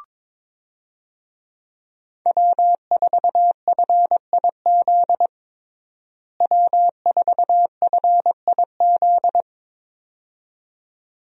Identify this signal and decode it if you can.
Morse code, heard as a keyed tone, about 22 wpm, 700 Hz. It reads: W4FIZ W4FIZ
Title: USA Call Signs - 5 Characters - Repeated 2x - 22wpm